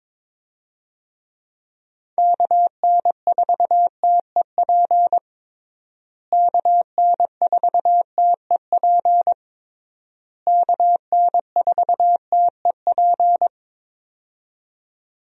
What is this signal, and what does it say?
Morse code, heard as a keyed tone, about 22 wpm, 700 Hz. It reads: KN4TEP KN4TEP KN4TEP
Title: USA Call Signs - 6 Characters - Repeated 3x - 22wpm